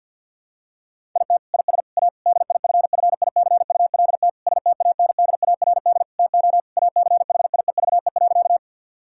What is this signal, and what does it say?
Morse code, heard as a keyed tone, about 50 wpm, 700 Hz. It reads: IT IS A DIFFICULT STANDARD TO ACHIEVE.
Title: Sentences from Top 1000 Words - Review Part 6 - 50wpm